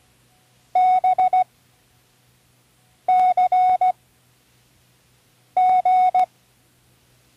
Morse code, heard as a keyed tone, about 16 wpm, spaced as for 5 wpm, 740 Hz. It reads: BCG
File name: CWResources / Learning Morse Code (from Tune in the World) / 19 random